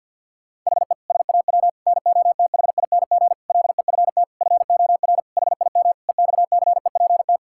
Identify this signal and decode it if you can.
Morse code, heard as a keyed tone, about 50 wpm, 700 Hz. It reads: HE SAW NOTHING LEFT FOR HIM EXCEPT
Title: Sentences from Top 1600 Words 50wpm